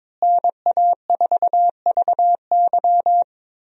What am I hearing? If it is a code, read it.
Morse code, heard as a keyed tone, about 22 wpm, 700 Hz. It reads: NA4VY
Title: USA Call Signs - 5 Characters - Repeated 2x - 22wpm